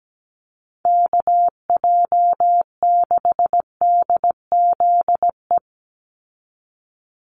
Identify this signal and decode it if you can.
Morse code, heard as a keyed tone, about 17 wpm, 700 Hz. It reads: KJ6DZE
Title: USA Call Signs 17wpm